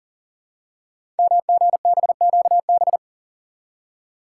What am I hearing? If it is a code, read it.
Morse code, heard as a keyed tone, about 38 wpm, 690 Hz. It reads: MGBQB
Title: Code Groups - 5 Letters - 40wpm